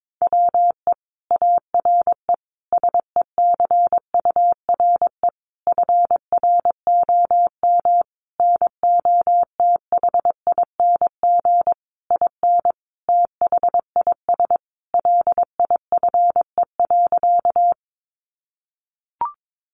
Morse code, heard as a keyed tone, about 22 wpm, 700 Hz. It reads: WE ARE SECURE FROM NOTHING IN THIS LIFE.
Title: Sentences from Top 1200 Words 22wpm